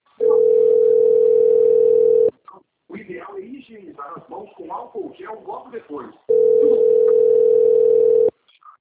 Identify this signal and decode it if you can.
Ringback tone (North America), 440 + 480 Hz, with speech.
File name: rg-1600-unknown- - - .12812